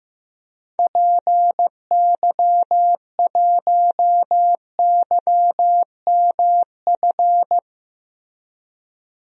Morse code, heard as a keyed tone, about 15 wpm, 700 Hz. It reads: PY1YMF
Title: DXCC-100 Call Signs - 6 Characters - Repeated 2x - 15wpm